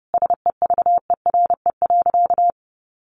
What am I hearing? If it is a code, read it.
Morse code, heard as a keyed tone, about 30 wpm, 700 Hz. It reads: SEVERE.
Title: Sentences from Top 1500 Words 30wpm